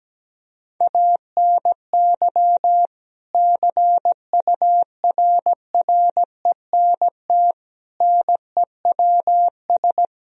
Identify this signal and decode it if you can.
Morse code, heard as a keyed tone, about 17 wpm, 700 Hz. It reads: ANY CURRENT NEWS
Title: Sentences from Top 1000 Words - Review Part 4 - 17wpm